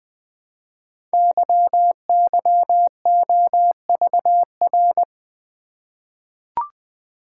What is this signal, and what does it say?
Morse code, heard as a keyed tone, about 20 wpm, 700 Hz. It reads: YYOVR
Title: Code Groups - 5 Letters - 20wpm